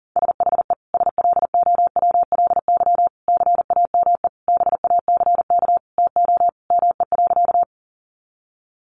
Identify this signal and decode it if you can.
Morse code, heard as a keyed tone, about 40 wpm, 700 Hz. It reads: SHE SLOWLY CAME BACK TO ME.
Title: Sentences from Top 1100 Words 40wpm